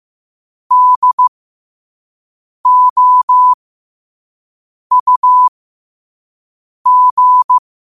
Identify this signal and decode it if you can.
Morse code, heard as a keyed tone, about 15 wpm, spaced as for 5 wpm, 1000 Hz. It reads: DOUG
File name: qotdfiles / 5wpm